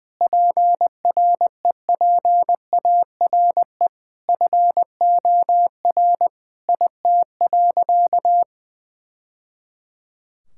Morse code, heard as a keyed tone, about 21 wpm, 700 Hz. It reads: PREPARE FOR IT.
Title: Sentences from Top 1000 Words - Review Part 6 - 20wpm